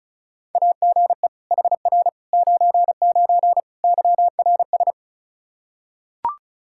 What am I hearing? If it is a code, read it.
Morse code, heard as a keyed tone, about 36 wpm, 700 Hz. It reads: AGE HR 99 YRS
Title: Practice QSO - Exchange 3 - More Info - 35wpm